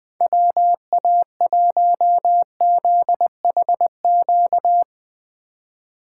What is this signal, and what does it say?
Morse code, heard as a keyed tone, about 21 wpm, 700 Hz. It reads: WA1ZHQ
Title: USA Call Signs - 6 Characters - 2x Character Spacing - 20wpm